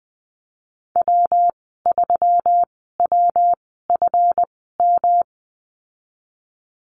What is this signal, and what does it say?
Morse code, heard as a keyed tone, about 20 wpm, spaced as for 14 wpm, 700 Hz. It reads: W3WFM
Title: 173 Lesson 28g - Number 9 - Call Signs 20wpm